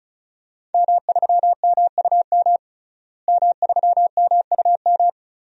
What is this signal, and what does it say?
Morse code, heard as a keyed tone, about 34 wpm, 700 Hz. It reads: M3MUM M3MUM
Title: DXCC-100 Call Signs - 5 Characters - Repeated 2x - 35wpm